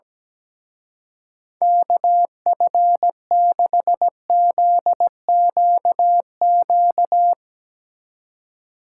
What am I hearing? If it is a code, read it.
Morse code, heard as a keyed tone, about 18 wpm, 700 Hz. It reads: KF6ZQQ
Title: Speed Racing - USA 2x3 Call Signs 17wpm